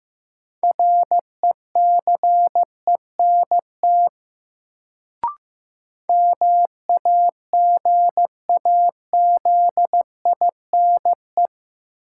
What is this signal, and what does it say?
Morse code, heard as a keyed tone, about 15 wpm, 700 Hz. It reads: RECENT MAGAZINE
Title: Sentences from Top 1000 Words - Review Part 7 - New Word Drill - 15wpm